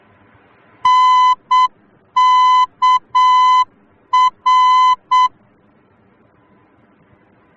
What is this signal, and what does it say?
Morse code, heard as a keyed tone, about 7 wpm, 1010 Hz. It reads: NKR